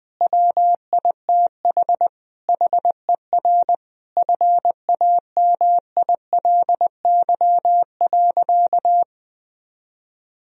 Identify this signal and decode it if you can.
Morse code, heard as a keyed tone, about 21 wpm, 700 Hz. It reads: WITH HER FAMILY.
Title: Sentences from Top 1000 Words - Review Part 4 - 20wpm